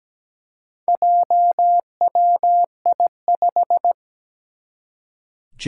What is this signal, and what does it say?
Morse code, heard as a keyed tone, about 17 wpm, 700 Hz. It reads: JWI5
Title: Four Challenging Letter-Number Combinations 17wpm